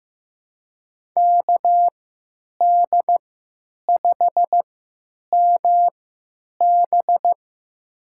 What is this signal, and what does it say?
Morse code, heard as a keyed tone, about 15 wpm, spaced as for 9 wpm, 700 Hz. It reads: KD5MB
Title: USA Call Signs - 5 Characters - 3x Character Spacing - 15wpm